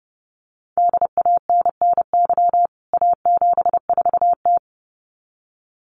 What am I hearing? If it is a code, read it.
Morse code, heard as a keyed tone, about 28 wpm, 700 Hz. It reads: DANNY A74T
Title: CWT Exchange 30wpm